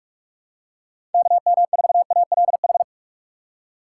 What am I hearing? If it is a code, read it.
Morse code, heard as a keyed tone, about 48 wpm, 690 Hz. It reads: KM4ALH
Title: USA Call Signs - 4x Character Spacing - 45wpm